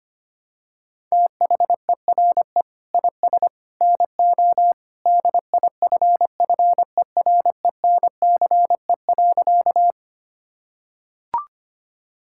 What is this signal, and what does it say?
Morse code, heard as a keyed tone, about 25 wpm, 700 Hz. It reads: THERE IS NO DIFFERENCE.
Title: Sentences from Top 1000 Words - Review Part 5 - 25wpm